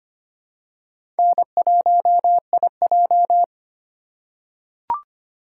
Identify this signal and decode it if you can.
Morse code, heard as a keyed tone, about 25 wpm, 700 Hz. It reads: N1IJ
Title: USA Call Signs - 4 Characters - 4x Character Spacing - 25wpm